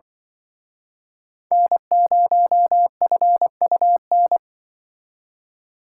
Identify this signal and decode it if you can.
Morse code, heard as a keyed tone, about 24 wpm, 700 Hz. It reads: N0FUN
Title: Speed Racing - USA Call Signs 20wpm